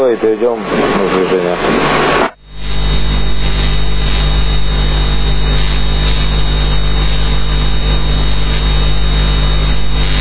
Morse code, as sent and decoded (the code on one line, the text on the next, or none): none